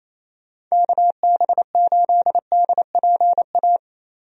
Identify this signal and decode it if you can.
Morse code, heard as a keyed tone, about 28 wpm, 700 Hz. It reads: KB8DPA
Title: USA Call Signs - 6 Characters - Repeated 2x - 28wpm